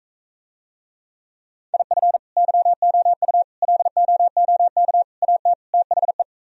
Morse code, heard as a keyed tone, about 45 wpm, 700 Hz. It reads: IF YOU LOOK AT THE